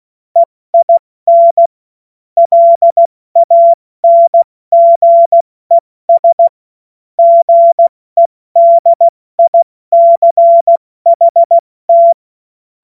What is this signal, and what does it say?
Morse code, heard as a keyed tone, about 16 wpm, spaced as for 14 wpm, 680 Hz. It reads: EIN LANGES GEDICHT
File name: Morsekurs / Witze / Witz 09. November 2021, 70 F80